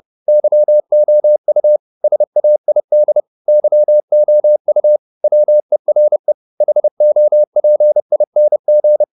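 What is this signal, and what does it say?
Morse code, heard as a keyed tone, about 30 wpm, 600 Hz. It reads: YOU SAID YOU WERE HOPING
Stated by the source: Morse code at 30 wpm, the novel Never by Ken Follett in English, from CW med Gnister.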